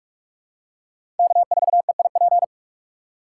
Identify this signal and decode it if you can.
Morse code, heard as a keyed tone, about 42 wpm, 690 Hz. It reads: K4EIP
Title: USA Call Signs - 5 Characters - 6x Character Spacing - 45wpm